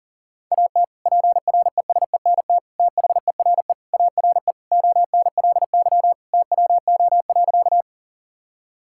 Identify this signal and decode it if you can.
Morse code, heard as a keyed tone, about 37 wpm, 700 Hz. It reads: AT PRESENT THERE ARE ONLY TWO.
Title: Sentences from Top 1000 Words - Review Part 5 - 40wpm